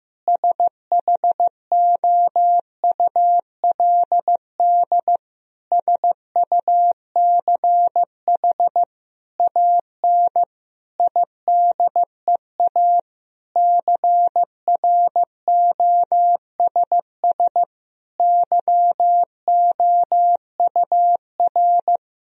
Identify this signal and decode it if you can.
Morse code, heard as a keyed tone, about 15 wpm, 700 Hz. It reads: SHOULD SUCH AN IDEA CROSS YOUR
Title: Sentences from Top 1000 Words - Review Part 1 - 15wpm